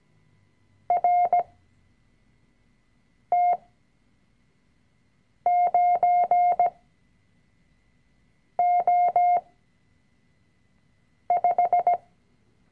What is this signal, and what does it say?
Morse code, heard as a keyed tone, about 17 wpm, spaced as for 4 wpm, 710 Hz. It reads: RT9O5
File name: CW-Online-Kurs / Anfänger / Lektion 005